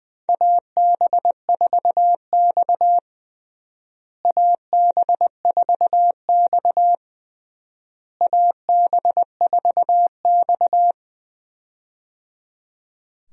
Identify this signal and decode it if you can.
Morse code, heard as a keyed tone, about 20 wpm, 700 Hz. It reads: AB4X AB4X AB4X
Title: USA Call Signs - 4 Characters - Repeated 3x - 20wpm